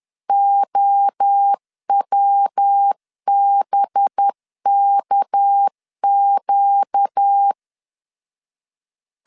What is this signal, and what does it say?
Morse code, heard as a keyed tone, about 10 wpm, 800 Hz. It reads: OWBKQ
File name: LEZIONI / 6 PKX0 / 6 PK 10wpm 2436